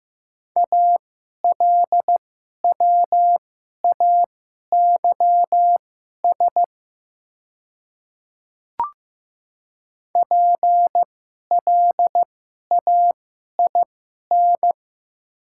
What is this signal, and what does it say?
Morse code, heard as a keyed tone, about 15 wpm, spaced as for 11 wpm, 700 Hz. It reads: ALWAYS PLAIN
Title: 2x Character Spacing - Top 500 Words - 15wpm